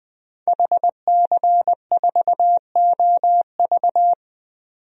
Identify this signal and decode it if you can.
Morse code, heard as a keyed tone, about 21 wpm, 700 Hz. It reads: HC4OV
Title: DXCC-100 Call Signs - 5 Characters - 20wpm